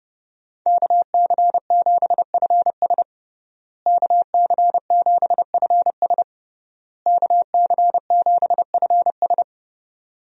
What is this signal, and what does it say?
Morse code, heard as a keyed tone, about 30 wpm, 700 Hz. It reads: KC7FS KC7FS KC7FS
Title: USA Call Signs - 5 Characters - Repeated 3x - 30wpm